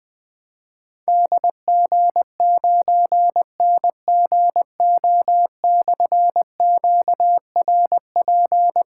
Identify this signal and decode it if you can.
Morse code, heard as a keyed tone, about 20 wpm, 700 Hz. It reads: DG9NGO/QRP
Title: Difficult Call Signs 20wpm